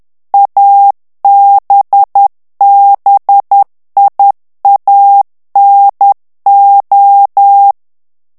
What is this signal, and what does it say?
Morse code, heard as a keyed tone, about 10 wpm, 800 Hz. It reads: ABBIANO
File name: ITALIEN texte clair 2 vitesse 10 mots minute